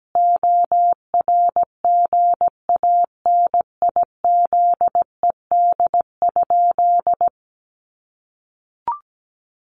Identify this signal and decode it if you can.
Morse code, heard as a keyed tone, about 17 wpm, 700 Hz. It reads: ORGANIZED?
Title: Sentences from Top 1300 Words 17wpm